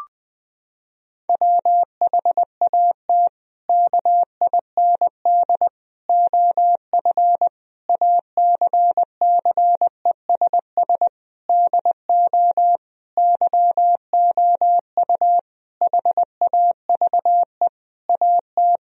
Morse code, heard as a keyed tone, about 21 wpm, 700 Hz. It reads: WHAT KIND OF ACCESS DO YOU HAVE AT
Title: Sentences from Top 1000 Words - Review Part 4 - 20wpm